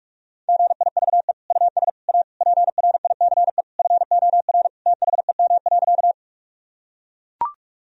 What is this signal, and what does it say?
Morse code, heard as a keyed tone, about 41 wpm, 700 Hz. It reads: GIVE US A PRICE FOR THEM.
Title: Sentences from Top 1000 Words - Review Part 4 - 45wpm